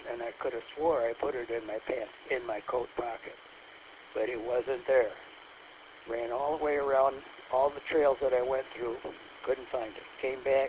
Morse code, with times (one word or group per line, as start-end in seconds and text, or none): none